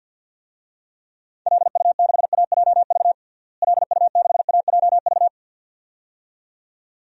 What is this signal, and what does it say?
Morse code, heard as a keyed tone, about 50 wpm, 700 Hz. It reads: LU6AJV LU6AJV
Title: DXCC-100 Call Signs - 6 Characters - Repeated 2x - 50wpm